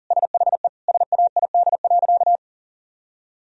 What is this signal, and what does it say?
Morse code, heard as a keyed tone, about 37 wpm, 700 Hz. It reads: SHE SAID.
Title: Sentences from Top 1000 Words - Review Part 1 - 40wpm